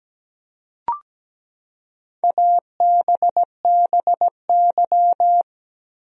Morse code, heard as a keyed tone, about 17 wpm, 700 Hz. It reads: ABBY
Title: USA Common Names - Max 4 Letters - 17wpm